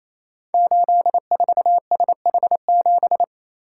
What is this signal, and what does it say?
Morse code, heard as a keyed tone, about 28 wpm, 700 Hz. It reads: 84SH7
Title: Code Groups - 5 Letter:Numbers - 28wpm